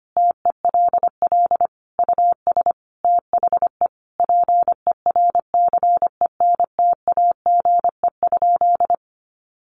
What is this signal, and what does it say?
Morse code, heard as a keyed tone, about 25 wpm, 700 Hz. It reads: TELL US THE PERCENTAGE?
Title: Sentences from Top 1300 Words 25wpm